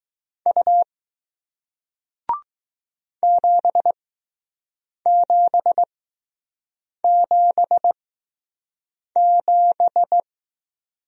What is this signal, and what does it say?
Morse code, heard as a keyed tone, about 20 wpm, spaced as for 6 wpm, 700 Hz. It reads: U 7777